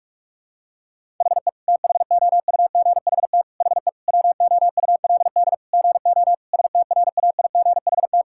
Morse code, heard as a keyed tone, about 46 wpm, 700 Hz. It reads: HE THOUGHT HE WOULD GO STRAIGHT
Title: Sentences from Top 1000 Words - Review Part 6 - 45wpm